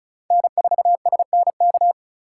Morse code, heard as a keyed tone, about 36 wpm, 700 Hz. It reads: N4SNK
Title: USA Call Signs - 5 Characters - 4x Character Spacing - 35wpm